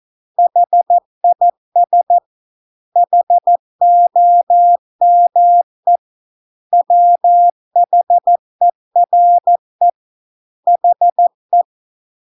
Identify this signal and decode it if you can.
Morse code, heard as a keyed tone, about 14 wpm, 700 Hz. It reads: HIS HOME WHERE HE